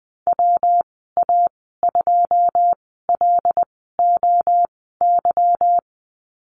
Morse code, heard as a keyed tone, about 20 wpm, spaced as for 15 wpm, 700 Hz. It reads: WA2LOY
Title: 154 Lesson 25g - Letter p - Call Signs 20wpm